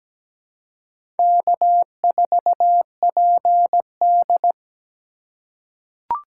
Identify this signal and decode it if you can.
Morse code, heard as a keyed tone, about 17 wpm, 700 Hz. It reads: K4PD
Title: USA Call Signs - 4 Characters - 4x Character Spacing - 17wpm